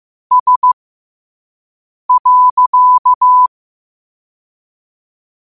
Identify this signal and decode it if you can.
Morse code, heard as a keyed tone, about 15 wpm, spaced as for 5 wpm, 1000 Hz. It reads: S.
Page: Quotes for Thu, 14 Aug 2025 in Morse Code at 5 words per minute.